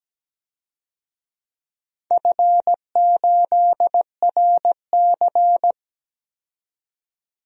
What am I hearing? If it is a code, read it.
Morse code, heard as a keyed tone, about 17 wpm, 700 Hz. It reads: F8RC
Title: DXCC-100 Call Signs - 4 Characters - 17wpm